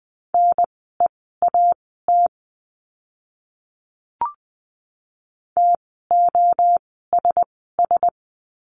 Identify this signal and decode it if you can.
Morse code, heard as a keyed tone, about 20 wpm, spaced as for 14 wpm, 700 Hz. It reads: NEAT TOSS
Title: 054 Lesson 10f - Letter r - Words 20wpm